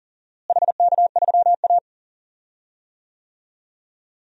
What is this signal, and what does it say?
Morse code, heard as a keyed tone, about 40 wpm, 700 Hz. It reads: HK3A
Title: DXCC-100 Call Signs - 4 Characters - Repeated 3x - 40wpm